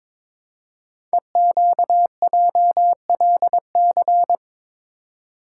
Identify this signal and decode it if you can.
Morse code, heard as a keyed tone, about 23 wpm, 700 Hz. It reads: EQJLC
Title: Code Groups - 5 Letter:Numbers - 22wpm